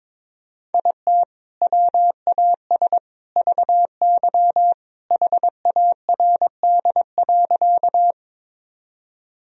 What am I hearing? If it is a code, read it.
Morse code, heard as a keyed tone, about 22 wpm, 700 Hz. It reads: IT WAS VY HARD.
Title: Sentences from Top 300 Words and Abbreviations 22wpm